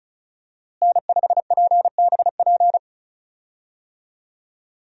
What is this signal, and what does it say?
Morse code, heard as a keyed tone, about 36 wpm, 700 Hz. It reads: N5PBP
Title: USA Call Signs - 5 Characters - Repeated 3x - 35wpm